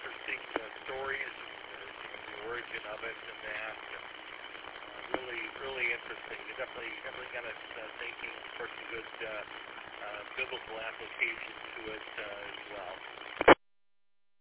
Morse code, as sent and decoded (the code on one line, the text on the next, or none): none